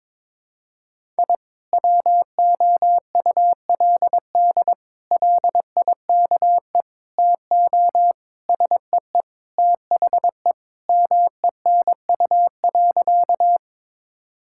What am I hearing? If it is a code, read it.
Morse code, heard as a keyed tone, about 22 wpm, 700 Hz. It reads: I WOULD LIKE TO SEE THE MENU.
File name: practice / 22wpm / Sentences with 4-Word Phrases - 22wpm